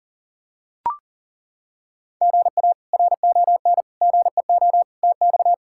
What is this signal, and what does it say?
Morse code, heard as a keyed tone, about 37 wpm, 700 Hz. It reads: GA RON GEO TX
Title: K1USN SST QSO Practice - 40wpm